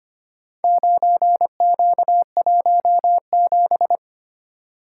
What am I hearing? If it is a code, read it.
Morse code, heard as a keyed tone, about 25 wpm, 700 Hz. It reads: 9Q17
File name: practice / 25wpm / Four Letter-Number Combinations - 25wpm